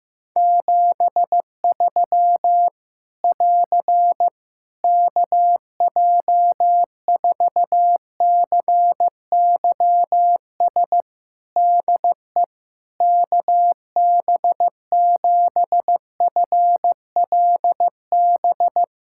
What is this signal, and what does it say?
Morse code, heard as a keyed tone, about 15 wpm, 700 Hz. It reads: 73 <AR> KJ4CYS DE KB7FLB
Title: Practice QSO - Exchange 4 - Ending - 15wpm